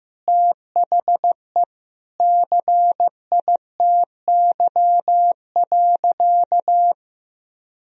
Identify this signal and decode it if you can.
Morse code, heard as a keyed tone, about 15 wpm, 700 Hz. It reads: THE CITY.
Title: Sentences from Top 1000 Words - Review Part 2 - 15wpm